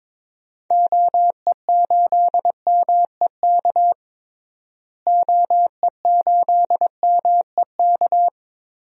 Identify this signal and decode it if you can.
Morse code, heard as a keyed tone, about 22 wpm, 700 Hz. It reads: OE8MEK OE8MEK
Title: DXCC-100 Call Signs - 6 Characters - Repeated 2x - 22wpm